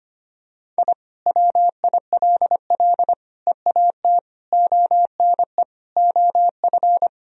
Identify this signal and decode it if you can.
Morse code, heard as a keyed tone, about 25 wpm, 700 Hz. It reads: I WILL EAT ONE OF
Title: Sentences from Top 1000 Words - Review Part 1 - 25wpm